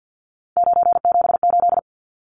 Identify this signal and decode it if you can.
Morse code, heard as a keyed tone, about 50 wpm, 700 Hz. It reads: 978
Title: Numbers - 3-Digit 50wpm